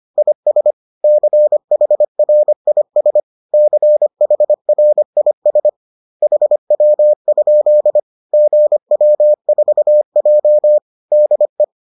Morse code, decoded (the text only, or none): IS CHRIS CHRIS HW? GW4J DE